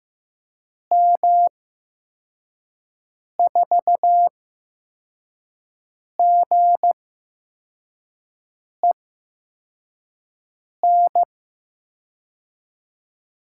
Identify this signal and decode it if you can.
Morse code, heard as a keyed tone, about 15 wpm, spaced as for 4 wpm, 700 Hz. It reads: M4GEN
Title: USA Call Signs - 6 Characters - 8x Character Spacing - 15wpm